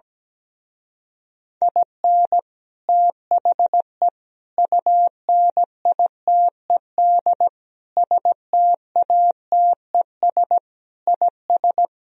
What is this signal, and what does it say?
Morse code, heard as a keyed tone, about 17 wpm, 700 Hz. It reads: IN THE UNITED STATES IS
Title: Speed-Racing - Sets of 5 Words 17wpm